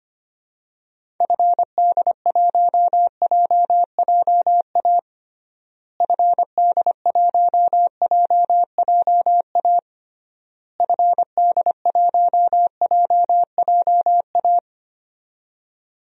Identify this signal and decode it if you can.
Morse code, heard as a keyed tone, about 25 wpm, 700 Hz. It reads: FD1JJA FD1JJA FD1JJA
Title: DXCC-100 Call Signs - 6 Characters - Repeated 3x - 25wpm